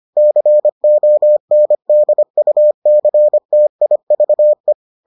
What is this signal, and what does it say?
Morse code, heard as a keyed tone, about 25 wpm, 600 Hz. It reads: CONDUCTIVE